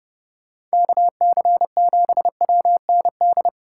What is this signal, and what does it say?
Morse code, heard as a keyed tone, about 29 wpm, 700 Hz. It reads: KC7WND